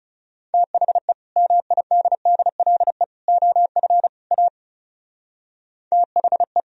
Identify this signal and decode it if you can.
Morse code, heard as a keyed tone, about 39 wpm, 700 Hz. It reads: THE MIDDLE OF A THE